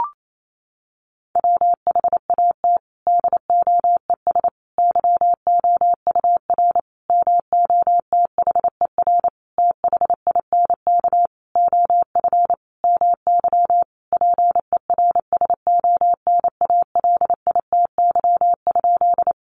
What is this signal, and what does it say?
Morse code, heard as a keyed tone, about 28 wpm, 700 Hz. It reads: WHAT DOES YOUR MOTHER THINK OF MY PERSONALITY?
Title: Sentences from Top 1400 Words 28wpm